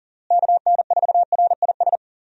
Morse code, heard as a keyed tone, about 37 wpm, 700 Hz. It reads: KN4RIS
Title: USA Call Signs - 6 Characters - Repeated 3x - 40wpm